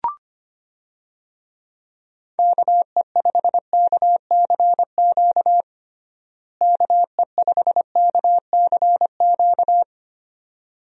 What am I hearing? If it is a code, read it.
Morse code, heard as a keyed tone, about 25 wpm, 700 Hz. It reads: KE5KCQ KE5KCQ
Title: USA Call Signs - 6 Characters - Repeated 2x - 25wpm